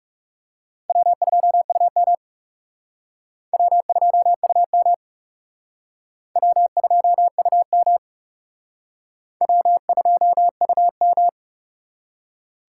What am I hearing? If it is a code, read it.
Morse code, heard as a keyed tone, about 36 wpm, 700 Hz. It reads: W2UM W2UM W2UM W2UM
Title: Speed Racing - USA 1x2 Call Signs 30wpm